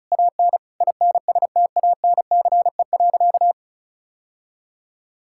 Morse code, heard as a keyed tone, about 36 wpm, 700 Hz. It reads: AN INSTANCE.
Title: Sentences from Top 1000 Words - Review Part 5 - 35wpm